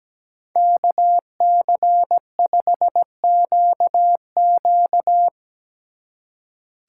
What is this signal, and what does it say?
Morse code, heard as a keyed tone, about 18 wpm, 700 Hz. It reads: KC5QQ
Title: Speed Racing - USA 2x2 Call Signs 17wpm